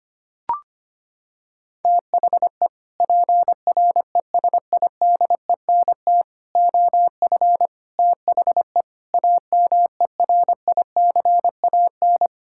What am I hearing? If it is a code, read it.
Morse code, heard as a keyed tone, about 25 wpm, 700 Hz. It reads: THE PRESIDENT OF THE AMERICAN
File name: practice / 25wpm / Sentences with 3-Word Phrases - 25wpm